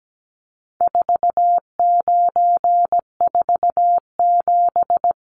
Morse code, heard as a keyed tone, about 17 wpm, 700 Hz. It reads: 4947
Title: Numbers - 4-Digit 17wpm